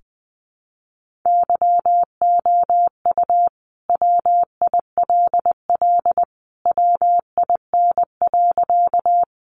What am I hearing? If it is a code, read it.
Morse code, heard as a keyed tone, about 20 wpm, 700 Hz. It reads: YOU WILL WIN.
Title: Sentences from Top 1500 Words 20wpm